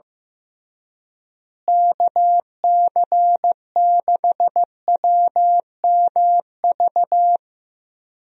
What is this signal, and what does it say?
Morse code, heard as a keyed tone, about 14 wpm, 700 Hz. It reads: KC6WMV
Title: USA Call Signs - 6 Characters - Repeated 3x - 15wpm